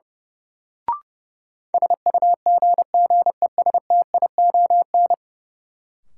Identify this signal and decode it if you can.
Morse code, heard as a keyed tone, about 32 wpm, 700 Hz. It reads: SUGGESTION
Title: Sentences from Top 1600 Words - New Word Drill Alt - 30wpm